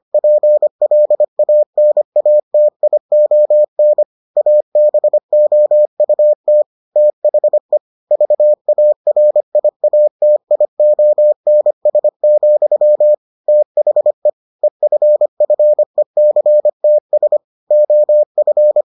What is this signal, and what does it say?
Morse code, heard as a keyed tone, about 25 wpm, 600 Hz. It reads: PLANATION ABOUT THE VARIATIONS, THE EFFECTS OF